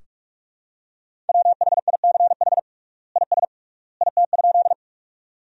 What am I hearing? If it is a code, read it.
Morse code, heard as a keyed tone, about 46 wpm, 700 Hz. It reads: WHICH IS IT?
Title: 3x Word-Spacing - Sentences from Top 100 Words - 45wpm